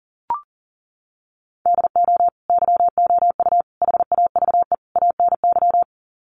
Morse code, heard as a keyed tone, about 40 wpm, 700 Hz. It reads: DO YOU HAVE ANY
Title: Sets of 4 Words 40wpm